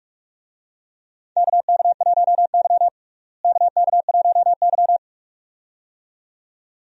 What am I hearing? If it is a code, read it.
Morse code, heard as a keyed tone, about 43 wpm, 700 Hz. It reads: KK1Y KK1Y
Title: USA Call Signs - 4 Characters - Repeated 2x - 45wpm